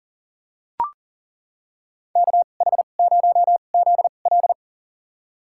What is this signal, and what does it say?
Morse code, heard as a keyed tone, about 39 wpm, spaced as for 29 wpm, 700 Hz. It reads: KH0ZL
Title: USA Call Signs - 5 Characters - 2x Character Spacing - 40wpm